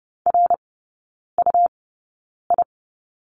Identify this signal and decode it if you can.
Morse code, heard as a keyed tone, about 30 wpm, spaced as for 9 wpm, 700 Hz. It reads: RUI
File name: Lesson 18e - Letter m - Three Characters 30wpm